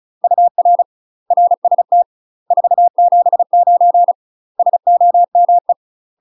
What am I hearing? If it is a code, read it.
Morse code, heard as a keyed tone, about 35 wpm, 700 Hz. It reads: UR RST 479 SOME
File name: UK MASTER QSOs TRAINING / 35WPM / UK MASTER QSOs 0038